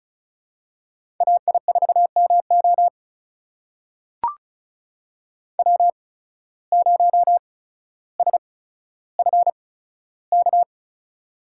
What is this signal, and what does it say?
Morse code, heard as a keyed tone, about 36 wpm, spaced as for 15 wpm, 700 Hz. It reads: AI4MO W0SFK
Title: USA Call Signs - 5 Characters - 8x Character Spacing - 35wpm